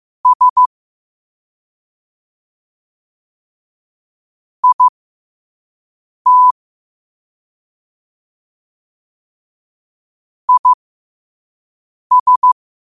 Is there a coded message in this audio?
Yes: Morse code from 0.3 to 12.5 s, heard as a keyed tone, about 15 wpm, spaced as for 5 wpm, 1000 Hz. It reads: S IT IS